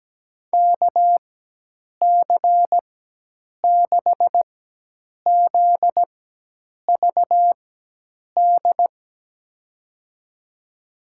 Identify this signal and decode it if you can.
Morse code, heard as a keyed tone, about 17 wpm, spaced as for 8 wpm, 700 Hz. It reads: KC6ZVD
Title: USA Call Signs - 6 Characters - 4x Character Spacing - 17wpm